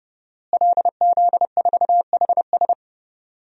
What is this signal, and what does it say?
Morse code, heard as a keyed tone, about 30 wpm, 700 Hz. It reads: LZ4HS
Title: DXCC-100 Call Signs - 5 Characters - Repeated 3x - 30wpm